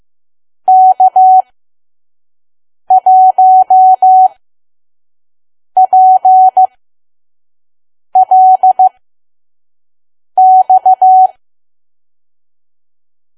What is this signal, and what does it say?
Morse code, heard as a keyed tone, about 15 wpm, spaced as for 5 wpm, 750 Hz. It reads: K1PLX
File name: Archive / 5 WPM files / PM